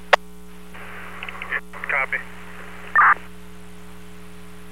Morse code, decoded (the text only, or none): none